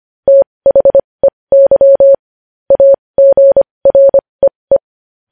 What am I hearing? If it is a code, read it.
Morse code, heard as a keyed tone, about 25 wpm, spaced as for 20 wpm, 550 Hz. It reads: THEY AGREE
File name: podcast / News headlines-1x-25x20